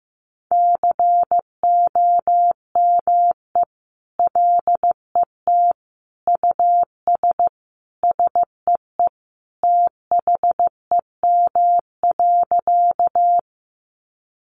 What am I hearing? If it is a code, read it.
Morse code, heard as a keyed tone, about 14 wpm, 700 Hz. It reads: COME LET US SEE THEM.
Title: Sentences from Top 300 Words 15wpm